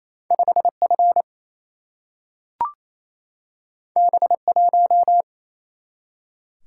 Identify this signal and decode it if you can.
Morse code, heard as a keyed tone, about 28 wpm, 700 Hz. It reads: 5F B1
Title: Single-Letter Single-Number 28wpm